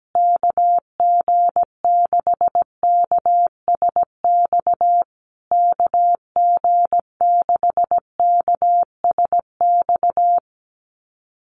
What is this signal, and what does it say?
Morse code, heard as a keyed tone, about 17 wpm, 700 Hz. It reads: KG6KSX KG6KSX
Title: USA Call Signs Repeated 3x 17wpm